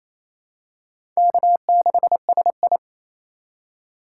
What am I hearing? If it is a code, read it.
Morse code, heard as a keyed tone, about 28 wpm, 700 Hz. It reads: K6SI
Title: USA Call Signs - 2x Character Spacing - 28wpm